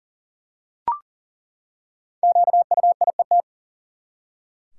Morse code, heard as a keyed tone, about 40 wpm, 700 Hz. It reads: QUIET
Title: Five Letter Words - 40wpm